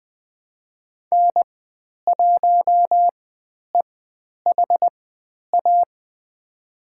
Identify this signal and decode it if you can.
Morse code, heard as a keyed tone, about 20 wpm, spaced as for 10 wpm, 700 Hz. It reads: N1EHA
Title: 109 Lesson 18g - Letter m - Call Signs 20wpm